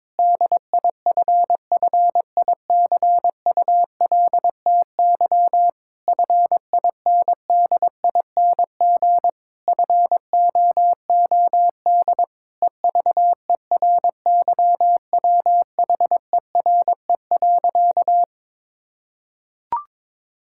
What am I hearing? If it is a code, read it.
Morse code, heard as a keyed tone, about 22 wpm, 700 Hz. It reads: DIFFICULTY FINDING FOOD EVERYWHERE.
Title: Sentences from Top 1600 Words 22wpm